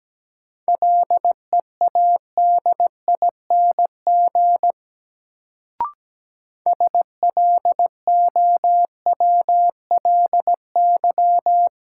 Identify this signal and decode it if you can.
Morse code, heard as a keyed tone, about 17 wpm, 700 Hz. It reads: LEADING SLOWLY
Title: Sentences from Top 1100 Words - New Word Drill Alt - 17wpm